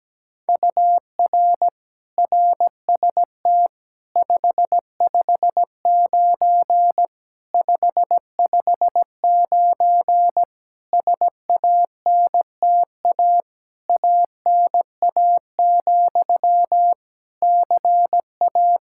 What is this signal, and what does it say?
Morse code, heard as a keyed tone, about 17 wpm, 700 Hz. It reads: UR RST 559 559 SANTA ANA, CA
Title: Practice QSO - Exchange 1 - Beginning - 17wpm